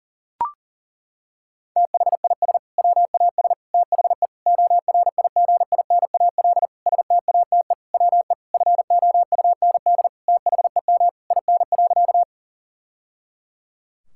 Morse code, heard as a keyed tone, about 40 wpm, 700 Hz. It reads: THIS WAS THE ORIGINAL STATE WE FOUND THEM IN.
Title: Sentences from Top 1000 Words - Review Part 6 - 40wpm